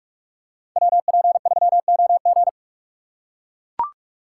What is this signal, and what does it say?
Morse code, heard as a keyed tone, about 42 wpm, 700 Hz. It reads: WP3OG
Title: USA Call Signs - 5 Characters - 4x Character Spacing - 45wpm